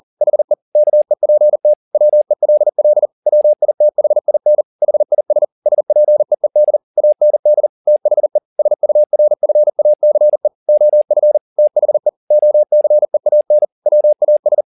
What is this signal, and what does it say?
Morse code, heard as a keyed tone, about 40 wpm, 600 Hz. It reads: HE KEPT WELL WITHIN HIS SPEED AND THE SURFACE OF THE OCEAN WAS